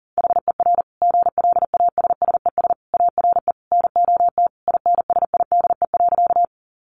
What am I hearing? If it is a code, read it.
Morse code, heard as a keyed tone, about 40 wpm, 700 Hz. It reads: HER GLASSES ARE NOT INSIDE.
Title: Additional Easy Phrases 40wpm